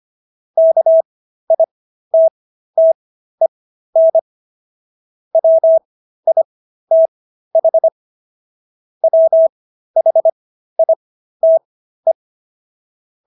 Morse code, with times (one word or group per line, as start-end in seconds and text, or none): 0.6-4.2 KITTEN
5.3-7.9 WITH
9.0-12.1 WHITE